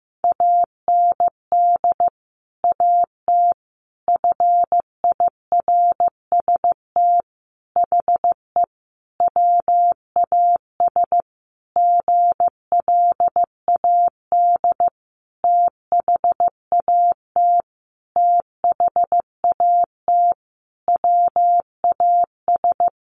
Morse code, heard as a keyed tone, about 15 wpm, 700 Hz. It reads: AND AT FIRST HE WAS GLAD THAT THAT WAS
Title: Sentences from Top 1300 Words 15wpm